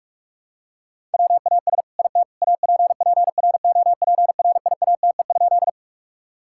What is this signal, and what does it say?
Morse code, heard as a keyed tone, about 41 wpm, 700 Hz. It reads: WAS IT APPROPRIATE?
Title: Sentences from Top 1000 Words - Review Part 5 - 45wpm